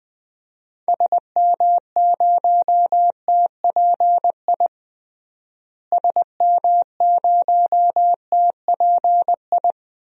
Morse code, heard as a keyed tone, about 21 wpm, 700 Hz. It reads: SM0TPI SM0TPI
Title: DXCC-100 Call Signs - 6 Characters - Repeated 2x - 20wpm